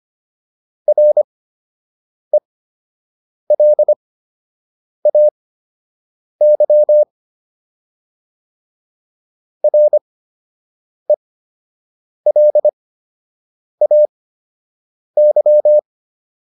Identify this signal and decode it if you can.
Morse code, heard as a keyed tone, about 25 wpm, spaced as for 7 wpm, 600 Hz. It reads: RELAY RELAY